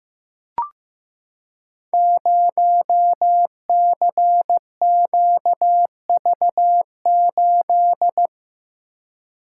Morse code, heard as a keyed tone, about 15 wpm, 700 Hz. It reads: 0CQV8
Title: Code Groups - 5 Letter:Numbers - 15wpm